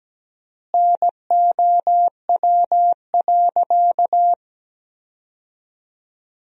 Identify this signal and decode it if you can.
Morse code, heard as a keyed tone, about 17 wpm, 700 Hz. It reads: NOW.
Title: 3x Word-Spacing - Sentences from Top 300 Words - 17wpm